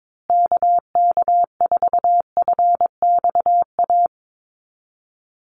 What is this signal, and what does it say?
Morse code, heard as a keyed tone, about 22 wpm, 700 Hz. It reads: KK4FXA
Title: USA Call Signs Repeated 2x 22wpm